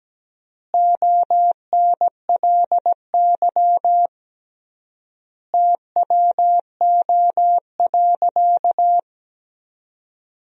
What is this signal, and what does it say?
Morse code, heard as a keyed tone, about 17 wpm, 700 Hz. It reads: ONLY TWO.
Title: 3x Word-Spacing - Sentences from Top 200 Words - 17wpm